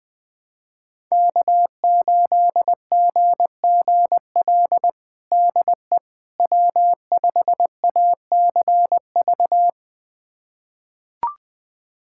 Morse code, heard as a keyed tone, about 20 wpm, 700 Hz. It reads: K8GGL DE W5ACV
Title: USA Callsign DE Callsign - 5 Characters - 20wpm